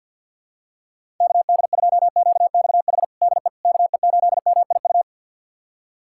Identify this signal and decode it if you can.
Morse code, heard as a keyed tone, about 52 wpm, 700 Hz. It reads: KD2QXH DE KE8MIU
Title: USA Callsign DE Callsign - 6 Characters - 50wpm